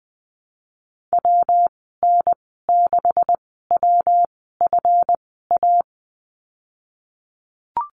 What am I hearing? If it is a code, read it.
Morse code, heard as a keyed tone, about 20 wpm, spaced as for 14 wpm, 700 Hz. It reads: WN6WFA
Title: 147 Lesson 24g - Letter y - Call Signs 20wpm